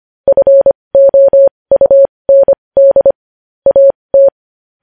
Morse code, heard as a keyed tone, about 25 wpm, spaced as for 20 wpm, 550 Hz. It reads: FOUND AT
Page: News Headlines in Morse Code at 20 WPM